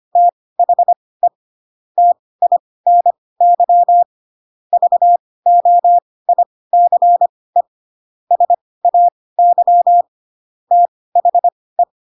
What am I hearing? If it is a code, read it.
Morse code, heard as a keyed tone, about 25 wpm, spaced as for 18 wpm, 700 Hz. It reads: THE TINY VOICE SAY THE
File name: Pinocchio / Pinocchio ch1 12 25 ik1hge